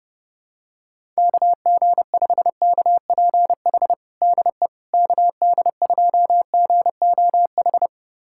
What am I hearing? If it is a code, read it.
Morse code, heard as a keyed tone, about 30 wpm, 700 Hz. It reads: KG5KPH DE KD2GOH
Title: USA Callsign DE Callsign 30wpm